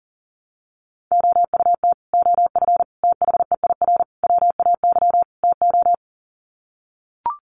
Morse code, heard as a keyed tone, about 40 wpm, 700 Hz. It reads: OUT OF THEIR WAY TO
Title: Sets of 5 Words Encore2 40wpm